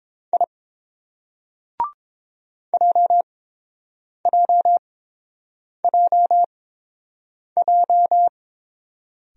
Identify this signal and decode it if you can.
Morse code, heard as a keyed tone, about 30 wpm, spaced as for 7 wpm, 700 Hz. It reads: I JJJJ